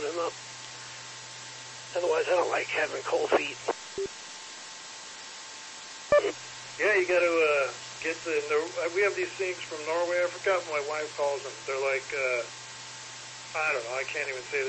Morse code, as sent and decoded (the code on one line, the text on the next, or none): none